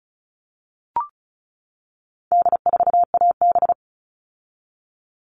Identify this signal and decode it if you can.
Morse code, heard as a keyed tone, about 34 wpm, 700 Hz. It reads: D4AB
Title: All Cut Numbers - 4-Digit 35wpm